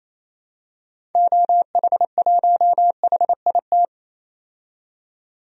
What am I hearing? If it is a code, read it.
Morse code, heard as a keyed tone, about 28 wpm, 700 Hz. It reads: OH1HIT
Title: DXCC-100 Call Signs - 6 Characters - 28wpm